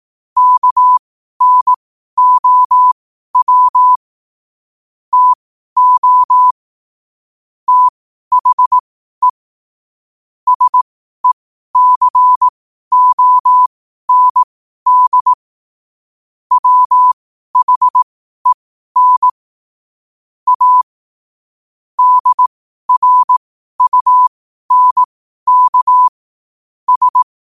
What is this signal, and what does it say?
Morse code, heard as a keyed tone, about 18 wpm, spaced as for 12 wpm, 1000 Hz. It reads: KNOW TO THE SECOND WHEN A DRUNK S
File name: qotdfiles / 12wpm